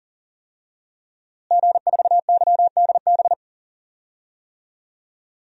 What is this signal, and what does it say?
Morse code, heard as a keyed tone, about 40 wpm, 700 Hz. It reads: G4YDB
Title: DXCC-100 Call Signs - 5 Characters - Repeated 2x - 40wpm